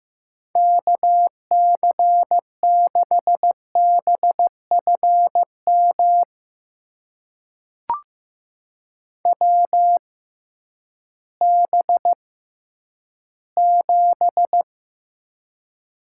Morse code, heard as a keyed tone, about 15 wpm, spaced as for 10 wpm, 700 Hz. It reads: KC6BFM WB7
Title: USA Call Signs - 6 Characters - 6x Character Spacing - 15wpm